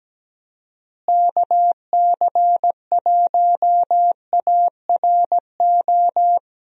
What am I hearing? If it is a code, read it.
Morse code, heard as a keyed tone, about 17 wpm, 700 Hz. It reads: KC1ARO
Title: USA Call Signs - 6 Characters - Repeated 3x - 17wpm